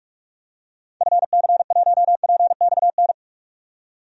Morse code, heard as a keyed tone, about 42 wpm, 700 Hz. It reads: FC1PXN
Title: DXCC-100 Call Signs - 6 Characters - 45wpm